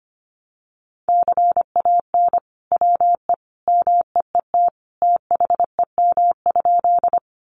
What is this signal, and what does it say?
Morse code, heard as a keyed tone, about 25 wpm, 700 Hz. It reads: CAN WE MEET THEM?
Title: Sentences from Top 1100 Words 25wpm